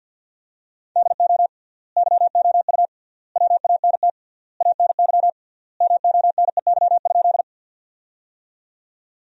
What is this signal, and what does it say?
Morse code, heard as a keyed tone, about 52 wpm, 700 Hz. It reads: DO YOU WANT ANY MONEY?
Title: 3x Word-Spacing - Sentences from Top 400 Words - 50wpm